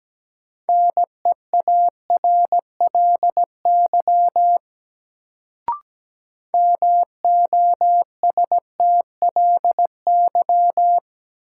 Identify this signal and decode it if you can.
Morse code, heard as a keyed tone, about 17 wpm, 700 Hz. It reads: NEARLY MOSTLY
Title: Sentences from Top 1300 Words - New Word Drill Alt - 17wpm